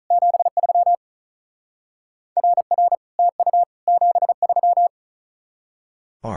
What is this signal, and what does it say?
Morse code, heard as a keyed tone, about 44 wpm, 700 Hz. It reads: 73 RR TU 73